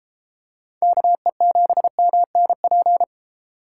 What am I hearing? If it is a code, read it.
Morse code, heard as a keyed tone, about 31 wpm, 690 Hz. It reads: KE7MNP